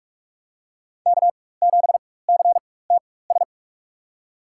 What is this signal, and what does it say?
Morse code, heard as a keyed tone, about 47 wpm, spaced as for 21 wpm, 700 Hz. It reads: K7CTS
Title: USA Call Signs - 5 Characters - 4x Character Spacing - 45wpm